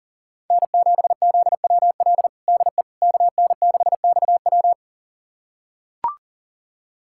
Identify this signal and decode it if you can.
Morse code, heard as a keyed tone, about 40 wpm, 700 Hz. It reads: N7ZWL DE KN6XW
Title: USA Callsign DE Callsign - 5 Characters - 40wpm